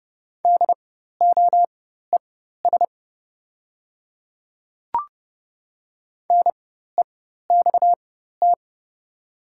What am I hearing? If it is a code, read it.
Morse code, heard as a keyed tone, about 29 wpm, spaced as for 14 wpm, 690 Hz. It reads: DOES NEXT
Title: 4x Character Spacing - Top 300 Words - 30wpm